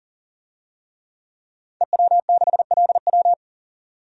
Morse code, heard as a keyed tone, about 40 wpm, 700 Hz. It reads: EW6LW
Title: DXCC-100 Call Signs - 5 Characters - 40wpm